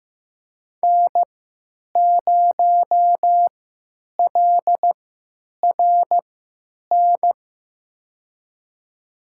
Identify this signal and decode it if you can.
Morse code, heard as a keyed tone, about 15 wpm, spaced as for 9 wpm, 700 Hz. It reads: N0LRN
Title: USA Call Signs - 5 Characters - 3x Character Spacing - 15wpm